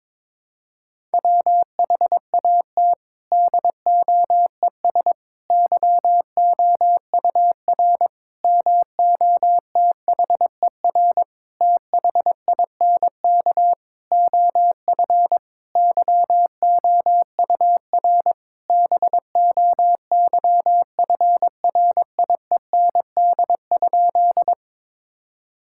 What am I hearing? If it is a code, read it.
Morse code, heard as a keyed tone, about 22 wpm, 700 Hz. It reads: WHAT DOES YOUR MOTHER THINK OF YOUR BOYFRIEND?
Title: Sentences from Top 1000 Words - Review Part 5 - 22wpm